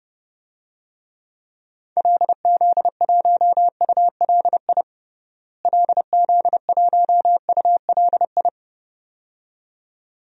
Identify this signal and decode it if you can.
Morse code, heard as a keyed tone, about 30 wpm, 700 Hz. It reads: LZ1ULI LZ1ULI
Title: DXCC-100 Call Signs - 6 Characters - Repeated 2x - 30wpm